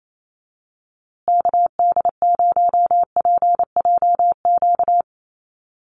Morse code, heard as a keyed tone, about 28 wpm, 700 Hz. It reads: KD0PJQ
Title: USA Call Signs Repeated 2x 28wpm